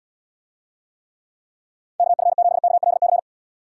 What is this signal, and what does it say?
Morse code, heard as a keyed tone, about 69 wpm, 700 Hz. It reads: KF7KLC